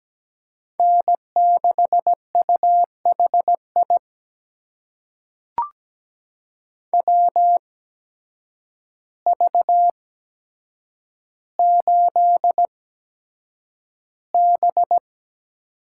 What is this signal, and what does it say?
Morse code, heard as a keyed tone, about 17 wpm, spaced as for 8 wpm, 700 Hz. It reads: N6UHI WV8B
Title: USA Call Signs - 5 Characters - 8x Character Spacing - 17wpm